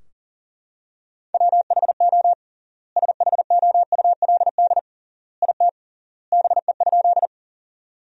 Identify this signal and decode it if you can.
Morse code, heard as a keyed tone, about 40 wpm, 700 Hz. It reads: WHO SHOULD IT BE?
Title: 3x Word-Spacing - Sentences from Top 300 Words - 40wpm